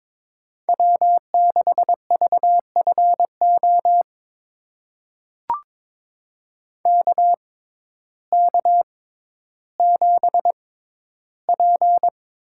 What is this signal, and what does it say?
Morse code, heard as a keyed tone, about 21 wpm, spaced as for 12 wpm, 700 Hz. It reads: W6VFO KK7P
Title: USA Call Signs - 5 Characters - 6x Character Spacing - 22wpm